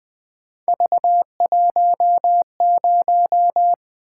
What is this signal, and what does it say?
Morse code, heard as a keyed tone, about 20 wpm, 700 Hz. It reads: V10
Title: Three Letter-Number Combinations - 20wpm